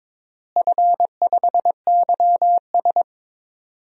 Morse code, heard as a keyed tone, about 21 wpm, 700 Hz. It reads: F5YS
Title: DXCC-100 Call Signs - 4 Characters - Repeated 2x - 22wpm